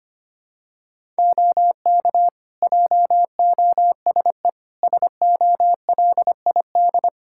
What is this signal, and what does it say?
Morse code, heard as a keyed tone, about 25 wpm, 700 Hz. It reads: OK JOSE SOLID
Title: K1USN SST QSO Practice 25wpm